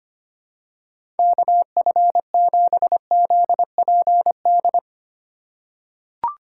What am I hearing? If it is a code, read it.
Morse code, heard as a keyed tone, about 25 wpm, 700 Hz. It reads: KF7ZPD
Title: USA Call Signs - 3x Character Spacing - 25wpm